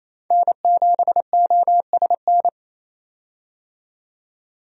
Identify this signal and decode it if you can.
Morse code, heard as a keyed tone, about 28 wpm, 700 Hz. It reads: N7OSN
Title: USA Call Signs - 5 Characters - 28wpm